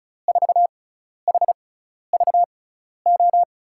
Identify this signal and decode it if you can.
Morse code, heard as a keyed tone, about 36 wpm, spaced as for 12 wpm, 690 Hz. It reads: 4HVO
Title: USA Call Signs - 6 Characters - 6x Character Spacing - 35wpm